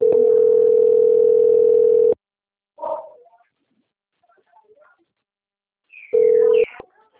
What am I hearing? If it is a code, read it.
Ringback tone (North America), 440 + 480 Hz.